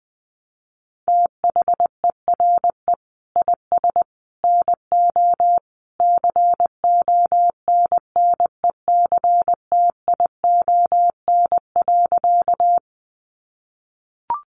Morse code, heard as a keyed tone, about 20 wpm, 700 Hz. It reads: THERE IS NO CONNECTION.
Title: Sentences from Top 1300 Words 20wpm